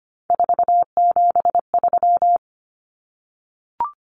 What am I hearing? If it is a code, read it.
Morse code, heard as a keyed tone, about 25 wpm, 700 Hz. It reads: 473
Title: Numbers - 3-Digit 25wpm